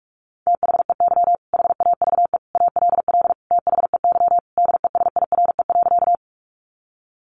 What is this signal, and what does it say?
Morse code, heard as a keyed tone, about 44 wpm, 700 Hz. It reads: THEY HAVE ALL THEY DESIRE.
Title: Sentences from Top 1100 Words 45wpm